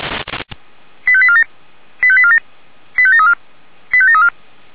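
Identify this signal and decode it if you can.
Five-tone selective call (ZVEI), heard as quick runs of five tones, tones of about 70 ms. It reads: 85847 85847 85734 85734
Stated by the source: Five-tone ZVEI selective call received with an SDR.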